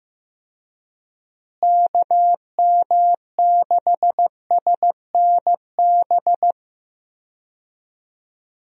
Morse code, heard as a keyed tone, about 15 wpm, 700 Hz. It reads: KM6SNB
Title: USA Call Signs - 6 Characters - Repeated 2x - 15wpm